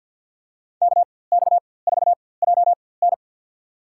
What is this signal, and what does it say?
Morse code, heard as a keyed tone, about 47 wpm, spaced as for 23 wpm, 700 Hz. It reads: KX4JN
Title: USA Call Signs - 5 Characters - 4x Character Spacing - 50wpm